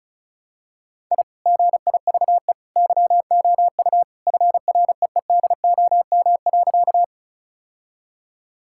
Morse code, heard as a keyed tone, about 36 wpm, 700 Hz. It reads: I GIVE YOU FREEDOM.
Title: Sentences from Top 1000 Words - Review Part 5 - 35wpm